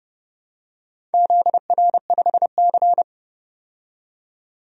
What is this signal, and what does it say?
Morse code, heard as a keyed tone, about 30 wpm, 700 Hz. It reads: ZR5C
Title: DXCC-100 Call Signs - 4 Characters - 30wpm